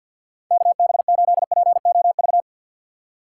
Morse code, heard as a keyed tone, about 47 wpm, 700 Hz. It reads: KB8POV
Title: USA Call Signs - 6 Characters - 3x Character Spacing - 50wpm